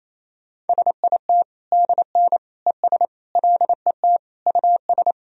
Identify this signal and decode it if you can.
Morse code, heard as a keyed tone, about 28 wpm, 700 Hz. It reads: SIT DN ES LET US
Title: Sentences from Top 500 Words and Abbreviations 28wpm